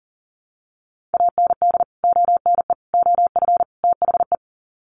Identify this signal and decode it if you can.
Morse code, heard as a keyed tone, about 40 wpm, 700 Hz. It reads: AND ONE OF THE
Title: Sets of 4 Words 40wpm